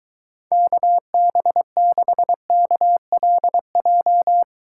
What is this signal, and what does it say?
Morse code, heard as a keyed tone, about 22 wpm, 700 Hz. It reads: KB6KLJ